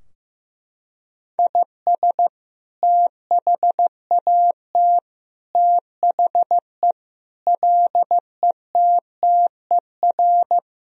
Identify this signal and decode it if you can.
Morse code, heard as a keyed tone, about 15 wpm, 700 Hz. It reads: IS THAT THE LETTER
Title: Sentences from Top 1000 Words - Review Part 2 - 15wpm